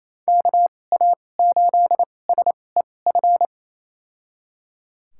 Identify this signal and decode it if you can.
Morse code, heard as a keyed tone, about 29 wpm, spaced as for 21 wpm, 700 Hz. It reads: KA8SEF
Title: USA Call Signs - 2x Character Spacing - 28wpm